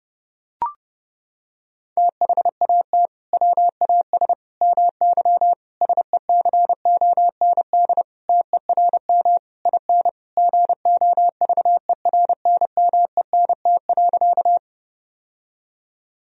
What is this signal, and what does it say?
Morse code, heard as a keyed tone, about 30 wpm, 700 Hz. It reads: THAT WAS MY SECOND TERM IN GOVERNMENT.
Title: Sentences from Top 1000 Words - Review Part 4 - 30wpm